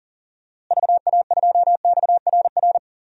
Morse code, heard as a keyed tone, about 40 wpm, 700 Hz. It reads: VA2XRR
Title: DXCC-100 Call Signs - 6 Characters - Repeated 3x - 40wpm